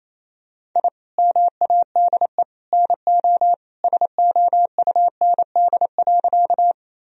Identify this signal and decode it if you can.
Morse code, heard as a keyed tone, about 27 wpm, 700 Hz. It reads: I MADE NO SOUND.
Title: Sentences from Top 1000 Words - Review Part 1 - 28wpm